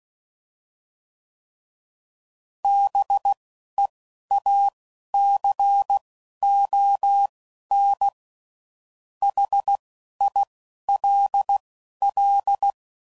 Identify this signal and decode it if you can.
Morse code, heard as a keyed tone, about 16 wpm, spaced as for 11 wpm, 800 Hz. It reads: BEACON HILL
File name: CW Code Groups @10WPM / fromSM's / Cities